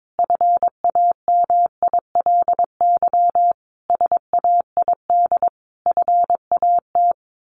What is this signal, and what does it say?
Morse code, heard as a keyed tone, about 22 wpm, 700 Hz. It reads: FAMILY SAID FAT
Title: Sentences from Top 1100 Words 22wpm